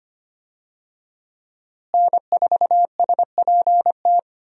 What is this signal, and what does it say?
Morse code, heard as a keyed tone, about 25 wpm, 700 Hz. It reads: N4SPT
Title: USA Call Signs - 5 Characters - Repeated 2x - 25wpm